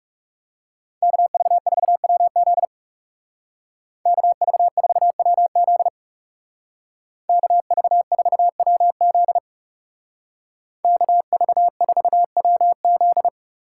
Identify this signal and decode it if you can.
Morse code, heard as a keyed tone, about 38 wpm, 700 Hz. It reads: KV4WZ KV4WZ KV4WZ KV4WZ